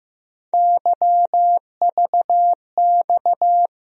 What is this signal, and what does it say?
Morse code, heard as a keyed tone, about 15 wpm, 700 Hz. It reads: YVX
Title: USA Call Sign Postfixes - 3 Letters 15wpm